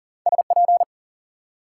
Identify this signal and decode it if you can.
Morse code, heard as a keyed tone, about 53 wpm, spaced as for 47 wpm, 700 Hz. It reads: SP
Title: Speed-Racing Multi-letter Phonemes - 40wpm